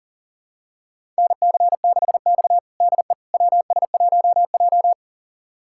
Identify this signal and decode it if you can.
Morse code, heard as a keyed tone, about 40 wpm, 700 Hz. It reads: NC6X DE WS1J
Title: USA Callsign DE Callsign - 4 Characters - 40wpm